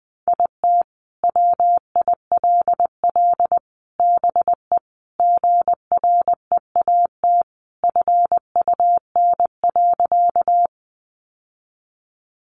Morse code, heard as a keyed tone, about 20 wpm, 700 Hz. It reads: IT WILL BE GREAT FUN.
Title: Sentences from Top 700 Words 20wpm